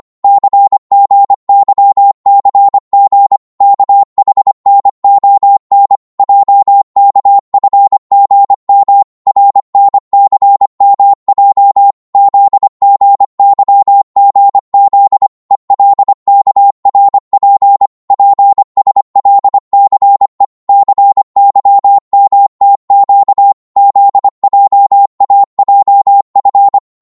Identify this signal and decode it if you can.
Morse code, heard as a keyed tone, about 25 wpm, 800 Hz. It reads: CGYCG KHNON JKFGM RNCMJ ZGYGZ ELKRP PSLCE CYMTQ ZJAJF
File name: Morse / ltr-25wpm